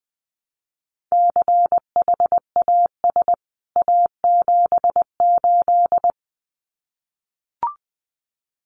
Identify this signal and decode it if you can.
Morse code, heard as a keyed tone, about 20 wpm, 700 Hz. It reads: CHAS A78
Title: CWT Exchange 20wpm